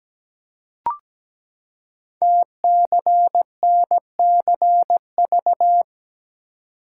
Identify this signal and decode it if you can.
Morse code, heard as a keyed tone, about 17 wpm, 700 Hz. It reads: TCNCV
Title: Code Groups - 5 Letters - 17wpm